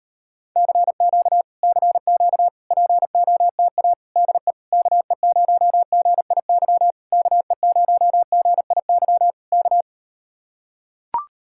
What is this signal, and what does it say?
Morse code, heard as a keyed tone, about 38 wpm, 700 Hz. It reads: CQ CQ POTA DE KE0GIY KE0GIY K